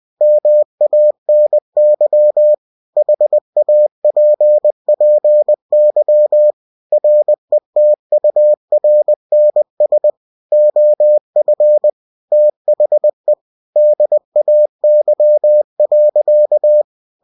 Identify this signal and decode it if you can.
Morse code, heard as a keyed tone, about 20 wpm, 600 Hz. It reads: MANY HAPPY RETURNS OF THE DAY.
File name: cw-books / winnie-the-pooh / winnie-20wpm-page-0056